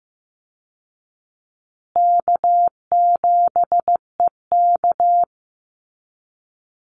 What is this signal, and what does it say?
Morse code, heard as a keyed tone, about 15 wpm, 700 Hz. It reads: K7EK
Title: Easy Call Signs Repeated 2x - 15wpm